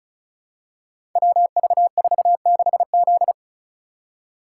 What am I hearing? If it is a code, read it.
Morse code, heard as a keyed tone, about 36 wpm, 700 Hz. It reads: WV46Z
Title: Code Groups - 5 Letter:Numbers - 35wpm